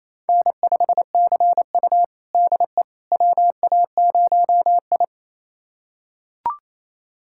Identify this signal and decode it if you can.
Morse code, heard as a keyed tone, about 28 wpm, 700 Hz. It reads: N5CU DE WA0I
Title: USA Callsign DE Callsign - 4 Characters - 28wpm